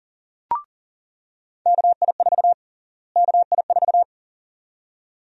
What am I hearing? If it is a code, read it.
Morse code, heard as a keyed tone, about 40 wpm, 700 Hz. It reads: KI4 KI4
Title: USA Call Sign Prefixes - Repeated 2x - 40wpm